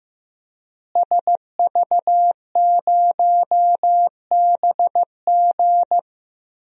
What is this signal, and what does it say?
Morse code, heard as a keyed tone, about 15 wpm, 700 Hz. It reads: SV0BG
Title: DXCC-100 Call Signs - 5 Characters - Repeated 3x - 15wpm